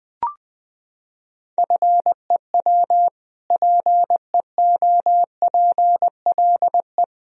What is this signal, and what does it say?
Morse code, heard as a keyed tone, about 21 wpm, 700 Hz. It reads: FEW PEOPLE
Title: Sentences from Top 1000 Words - Review Part 1 - 20wpm